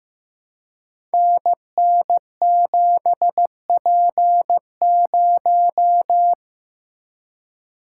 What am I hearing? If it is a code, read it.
Morse code, heard as a keyed tone, about 15 wpm, 700 Hz. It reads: NN7P0
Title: Code Groups - 5 Letter:Numbers - 15wpm